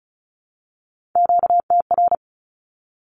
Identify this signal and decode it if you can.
Morse code, heard as a keyed tone, about 34 wpm, 700 Hz. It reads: QTR
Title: Q-Codes 35wpm